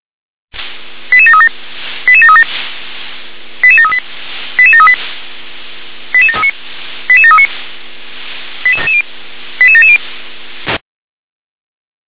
Five-tone selective call (ZVEI), heard as quick runs of five tones, tones of about 70 ms. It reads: 88737 88737 88738 88738 88739 88739 88770